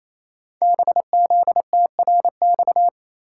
Morse code, heard as a keyed tone, about 27 wpm, 690 Hz. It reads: BZTRX
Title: Code Groups - 5 Letter:Numbers - 28wpm